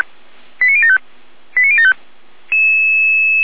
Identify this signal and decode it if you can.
Five-tone selective call (ZVEI), heard as quick runs of five tones, tones of about 70 ms. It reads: 89075 89075